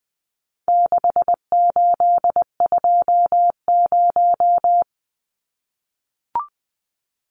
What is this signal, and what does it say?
Morse code, heard as a keyed tone, about 20 wpm, 700 Hz. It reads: 6820
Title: Numbers - 4-Digit 20wpm